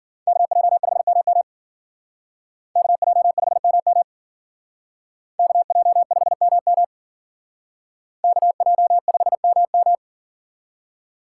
Morse code, heard as a keyed tone, about 51 wpm, 700 Hz. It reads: KJ5MM KJ5MM KJ5MM KJ5MM